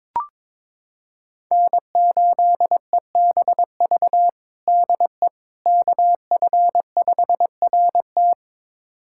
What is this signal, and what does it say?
Morse code, heard as a keyed tone, about 21 wpm, 700 Hz. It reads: N8EBV DE KF5RT
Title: USA Callsign DE Callsign - 5 Characters - 22wpm